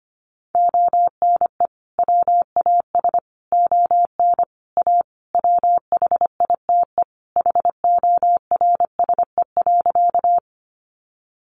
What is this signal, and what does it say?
Morse code, heard as a keyed tone, about 25 wpm, 700 Hz. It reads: ONE WAS ON A WHITE HORSE.
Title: Sentences from Top 300 Words 25wpm